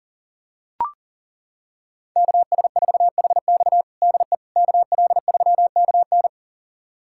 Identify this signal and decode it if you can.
Morse code, heard as a keyed tone, about 40 wpm, 700 Hz. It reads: KS4HX DE KL3KN
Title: USA Callsign DE Callsign - 5 Characters - 40wpm